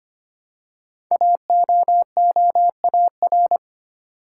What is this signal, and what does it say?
Morse code, heard as a keyed tone, about 25 wpm, 700 Hz. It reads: AOOAR
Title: Code Groups - 5 Letter:Numbers - 25wpm